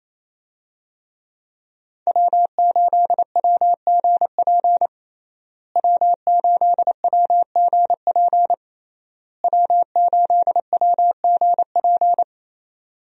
Morse code, heard as a keyed tone, about 28 wpm, 700 Hz. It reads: W8WGP W8WGP W8WGP
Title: USA Call Signs - 5 Characters - Repeated 3x - 28wpm